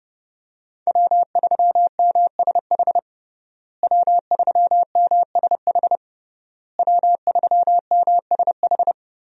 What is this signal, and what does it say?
Morse code, heard as a keyed tone, about 30 wpm, 700 Hz. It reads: W3MSH W3MSH W3MSH
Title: USA Call Signs - 5 Characters - Repeated 3x - 30wpm